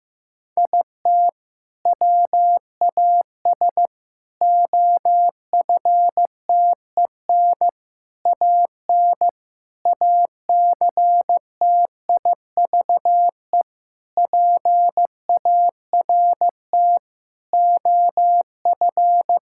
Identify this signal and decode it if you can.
Morse code, heard as a keyed tone, about 16 wpm, 700 Hz. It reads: IT WAS OFTEN AN ACTIVE PART OF
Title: Sentences from Top 1000 Words - Review Part 6 - 15wpm